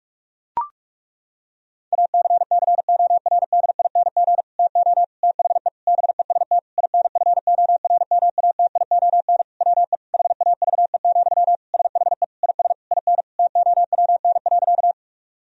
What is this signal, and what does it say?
Morse code, heard as a keyed tone, about 41 wpm, 700 Hz. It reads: ACCORDING TO THE BEST INFORMATION WE HAVE, SHE IS IN TOWN.
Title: Sentences from Top 1000 Words - Review Part 4 - 45wpm